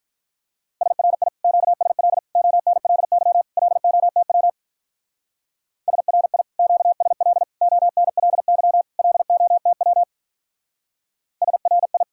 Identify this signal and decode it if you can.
Morse code, heard as a keyed tone, about 48 wpm, 700 Hz. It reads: SRI QSL ONLY LOTW SRI QSL ONLY LOTW SRI